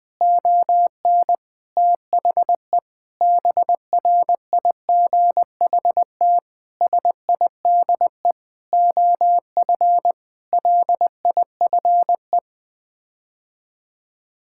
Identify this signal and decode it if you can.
Morse code, heard as a keyed tone, about 20 wpm, 700 Hz. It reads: ON THE BRIGHT SIDE OF LIFE
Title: Sets of 8 Words - 20wpm